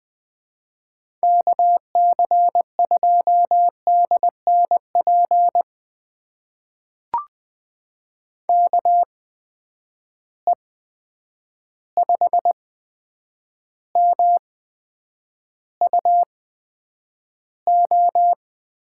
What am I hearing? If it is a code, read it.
Morse code, heard as a keyed tone, about 20 wpm, spaced as for 9 wpm, 700 Hz. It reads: KC2DNP KE5MUO
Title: USA Call Signs - 6 Characters - 8x Character Spacing - 20wpm